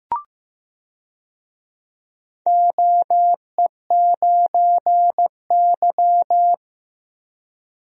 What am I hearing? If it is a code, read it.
Morse code, heard as a keyed tone, about 15 wpm, 700 Hz. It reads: OE9Y
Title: DXCC-100 Call Signs - 4 Characters - Repeated 2x - 15wpm